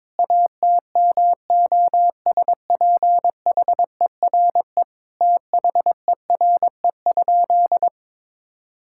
Morse code, heard as a keyed tone, about 22 wpm, 700 Hz. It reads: ATMOSPHERE THERE?
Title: Sentences from Top 1600 Words 22wpm